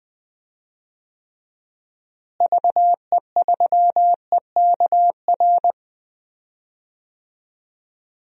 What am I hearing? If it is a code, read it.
Morse code, heard as a keyed tone, about 19 wpm, 700 Hz. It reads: VE3EKR
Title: DXCC-100 Call Signs - 6 Characters - Repeated 3x - 20wpm